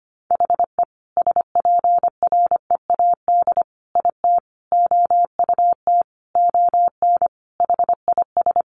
Morse code, heard as a keyed tone, about 25 wpm, 700 Hz. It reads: HE SPREAD IT OUT ON HIS
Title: Sentences from Top 1200 Words 25wpm